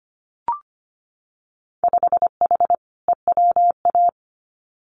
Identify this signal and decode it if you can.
Morse code, heard as a keyed tone, about 25 wpm, 690 Hz. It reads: 5H EWA
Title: Winter Field Day Exchanges Encore 25wpm